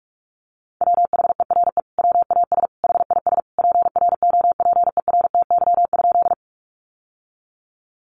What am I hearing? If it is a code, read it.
Morse code, heard as a keyed tone, about 44 wpm, 700 Hz. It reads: WHERE WAS HIS PROPERTY?
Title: Sentences from Top 1000 Words 45wpm